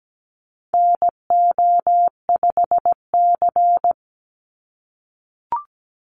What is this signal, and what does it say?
Morse code, heard as a keyed tone, about 17 wpm, 700 Hz. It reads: NO5C
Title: USA Call Signs 17wpm